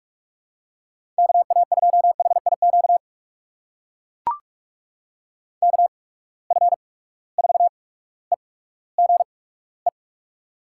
Morse code, heard as a keyed tone, about 42 wpm, spaced as for 19 wpm, 700 Hz. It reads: KA2HIQ KF4EGE
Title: USA Call Signs - 8x Character Spacing - 45wpm